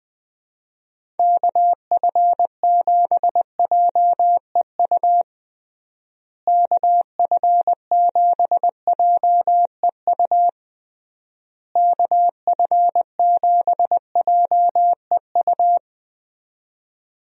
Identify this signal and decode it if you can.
Morse code, heard as a keyed tone, about 21 wpm, 700 Hz. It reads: KF7JEU KF7JEU KF7JEU
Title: USA Call Signs - 6 Characters - Repeated 3x - 20wpm